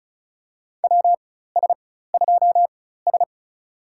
Morse code, heard as a keyed tone, about 36 wpm, spaced as for 16 wpm, 700 Hz. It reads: WS2S
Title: USA Call Signs - 4x Character Spacing - 35wpm